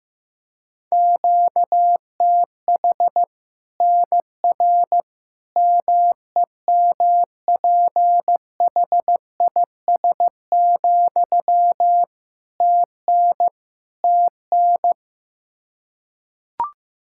Morse code, heard as a keyed tone, about 15 wpm, 700 Hz. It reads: QTH NR MEMPHIS, TN TN
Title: Practice QSO - Exchange 1 - Beginning - 15wpm